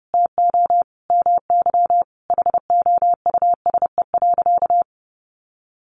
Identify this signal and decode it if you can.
Morse code, heard as a keyed tone, about 30 wpm, 700 Hz. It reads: TO MY HOUSE.
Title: Sentences from Top 1100 Words 30wpm